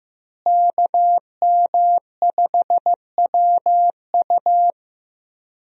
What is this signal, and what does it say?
Morse code, heard as a keyed tone, about 15 wpm, 700 Hz. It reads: KM5WU
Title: USA Call Signs - 5 Characters - Repeated 2x - 15wpm